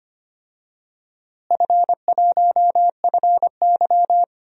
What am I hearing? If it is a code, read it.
Morse code, heard as a keyed tone, about 25 wpm, 700 Hz. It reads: F1FY
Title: DXCC-100 Call Signs - 4 Characters - Repeated 3x - 25wpm